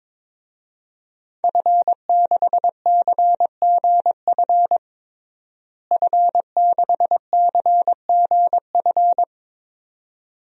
Morse code, heard as a keyed tone, about 22 wpm, 700 Hz. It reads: F6CGF F6CGF
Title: DXCC-100 Call Signs - 5 Characters - Repeated 2x - 22wpm